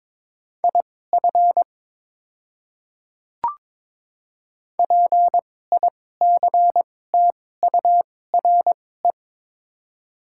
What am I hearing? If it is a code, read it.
Morse code, heard as a keyed tone, about 22 wpm, spaced as for 16 wpm, 700 Hz. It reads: IF PICTURE
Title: 2x Character Spacing - Top 300 Words - 22wpm